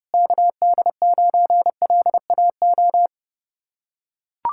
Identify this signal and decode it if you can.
Morse code, heard as a keyed tone, about 30 wpm, 700 Hz. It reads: KD9LAO